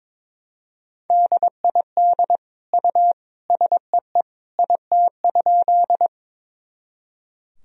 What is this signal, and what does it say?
Morse code, heard as a keyed tone, about 22 wpm, 700 Hz. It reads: DID U SEE IT?
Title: Sentences from Top 100 Words and Abbreviations 22wpm